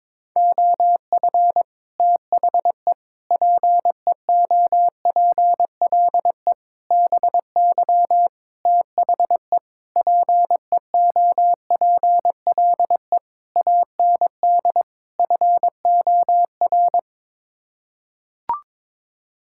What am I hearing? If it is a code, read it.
Morse code, heard as a keyed tone, about 21 wpm, 700 Hz. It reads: OF THE PEOPLE BY THE PEOPLE AND FOR
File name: practice / 22wpm / Sets of 8 Words - 22wpm